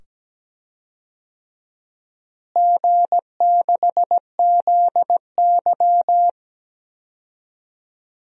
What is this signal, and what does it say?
Morse code, heard as a keyed tone, about 17 wpm, 700 Hz. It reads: G6ZY
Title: DXCC-100 Call Signs - 4 Characters - Repeated 2x - 17wpm